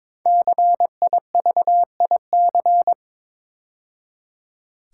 Morse code, heard as a keyed tone, about 22 wpm, 700 Hz. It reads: CIVIC
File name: 5 Letter Words - First 3 Letters Given - 22wpm